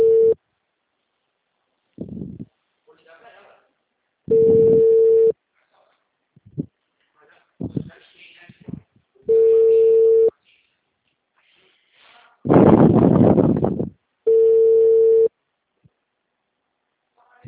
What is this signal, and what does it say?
Ringback tone (China, 450 Hz), with speech.
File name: out- -2022- - - .45724